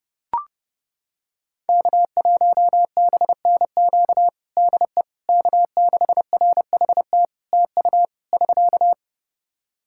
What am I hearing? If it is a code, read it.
Morse code, heard as a keyed tone, about 30 wpm, 700 Hz. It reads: K1BNQ DE K6RHT TU <SK>
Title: K1USN SST QSO Practice 30wpm